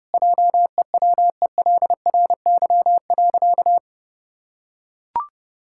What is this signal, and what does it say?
Morse code, heard as a keyed tone, about 30 wpm, 700 Hz. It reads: JEWELRY.
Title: Pangram Challenge v2 - 30wpm